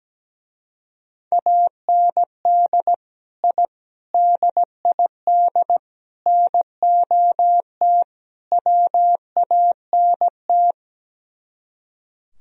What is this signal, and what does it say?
Morse code, heard as a keyed tone, about 17 wpm, 700 Hz. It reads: AND I DID NOT WANT
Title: Speed-Racing - Sets of 5 Words 17wpm